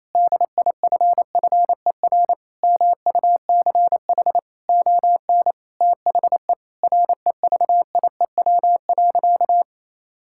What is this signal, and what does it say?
Morse code, heard as a keyed tone, about 28 wpm, 700 Hz. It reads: DIFFER MUCH ON THE REVIEW.
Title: Sentences from Top 1000 Words - Review Part 6 - 28wpm